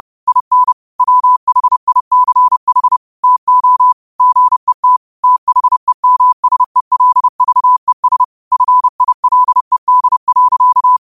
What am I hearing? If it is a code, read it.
Morse code, heard as a keyed tone, about 30 wpm, 1000 Hz. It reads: IN WHICH TO GET THEMSELVES FILED.
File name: qotdfiles / 30wpm